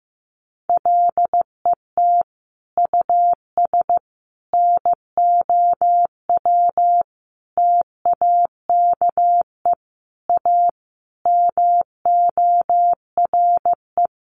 Morse code, heard as a keyed tone, about 14 wpm, 700 Hz. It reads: LET US NOW TAKE A MORE
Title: Sentences from Top 900 Words 15wpm